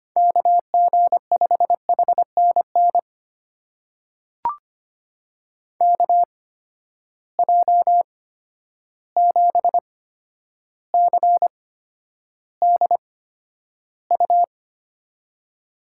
Morse code, heard as a keyed tone, about 25 wpm, spaced as for 11 wpm, 700 Hz. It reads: KG5HNN KJ7CDU
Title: USA Call Signs - 6 Characters - 8x Character Spacing - 25wpm